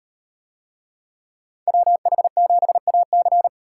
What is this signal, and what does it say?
Morse code, heard as a keyed tone, about 36 wpm, 700 Hz. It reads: WH7AC